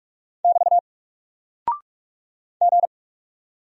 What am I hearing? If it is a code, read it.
Morse code, heard as a keyed tone, about 44 wpm, 700 Hz. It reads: <BT> G
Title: Single Character (Letter:Number:Punctuation:Prosign) Alt 45wpm